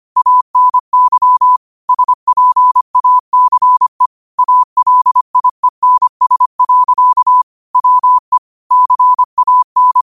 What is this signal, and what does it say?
Morse code, heard as a keyed tone, about 25 wpm, 1000 Hz. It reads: ANY SPACE ALIENS. WE CAN